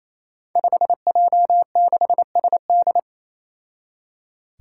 Morse code, heard as a keyed tone, about 28 wpm, 700 Hz. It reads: 5J6SD
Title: Five Challenging Letter-Number Combinations 28wpm